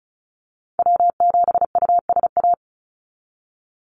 Morse code, heard as a keyed tone, about 34 wpm, 700 Hz. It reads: W7USA
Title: NA SOTA Activator Callsigns 35wpm